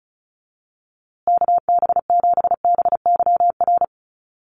Morse code, heard as a keyed tone, about 34 wpm, 700 Hz. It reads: KB7BYR
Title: USA Call Signs Repeated 2x 35wpm